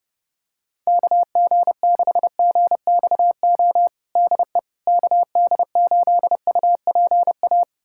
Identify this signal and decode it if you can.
Morse code, heard as a keyed tone, about 30 wpm, 700 Hz. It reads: KG6GXO DE KD8UPA
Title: USA Callsign DE Callsign - 6 Characters - 30wpm